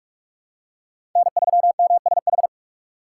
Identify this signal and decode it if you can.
Morse code, heard as a keyed tone, about 44 wpm, 700 Hz. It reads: N3MSH
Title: USA Call Signs - 3x Character Spacing - 45wpm